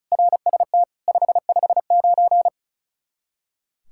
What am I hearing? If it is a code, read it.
Morse code, heard as a keyed tone, about 34 wpm, 700 Hz. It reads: RST 559
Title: Practice QSO - Exchange 1 - Beginning - 35wpm